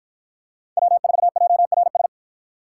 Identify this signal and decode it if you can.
Morse code, heard as a keyed tone, about 50 wpm, 700 Hz. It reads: W4JRS